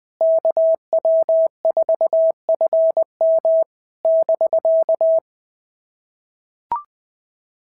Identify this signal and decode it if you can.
Morse code, heard as a keyed tone, about 18 wpm, 650 Hz. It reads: KW4FM <BK>
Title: POTA QSO Practice - 20wpm